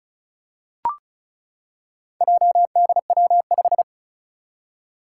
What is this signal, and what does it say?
Morse code, heard as a keyed tone, about 36 wpm, 700 Hz. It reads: JDW5
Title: Four Challenging Letter-Number Combinations 35wpm